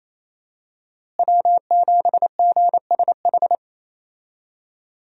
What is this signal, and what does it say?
Morse code, heard as a keyed tone, about 27 wpm, 700 Hz. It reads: W7GSH
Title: USA Call Signs - 5 Characters - Repeated 2x - 28wpm